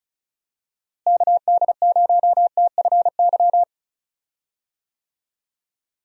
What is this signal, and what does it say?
Morse code, heard as a keyed tone, about 36 wpm, 700 Hz. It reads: KD0TFY
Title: USA Call Signs - 6 Characters - Repeated 2x - 35wpm